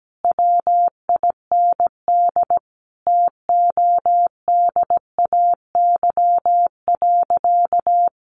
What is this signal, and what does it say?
Morse code, heard as a keyed tone, about 17 wpm, 700 Hz. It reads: WIND TODAY.
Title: Sentences from Top 1400 Words 17wpm